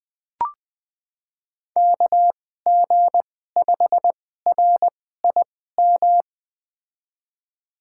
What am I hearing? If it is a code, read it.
Morse code, heard as a keyed tone, about 20 wpm, spaced as for 14 wpm, 700 Hz. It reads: KG5RIM
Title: USA Call Signs - 6 Characters - 2x Character Spacing - 20wpm